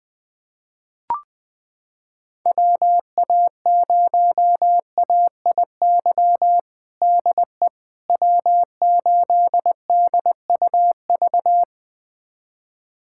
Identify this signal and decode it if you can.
Morse code, heard as a keyed tone, about 21 wpm, 700 Hz. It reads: WA0AIY DE W8DUV
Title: USA Callsign DE Callsign 20wpm